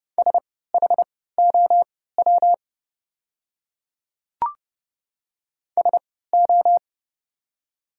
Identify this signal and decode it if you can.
Morse code, heard as a keyed tone, about 30 wpm, spaced as for 17 wpm, 700 Hz. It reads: SHOW SO
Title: 3x Character Spacing - Top 200 Words - 30wpm